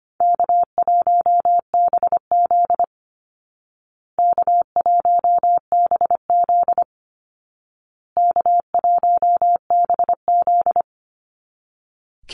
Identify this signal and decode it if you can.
Morse code, heard as a keyed tone, about 25 wpm, 700 Hz. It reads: K1BZ K1BZ K1BZ
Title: Easy Call Signs Repeated 3x - 25wpm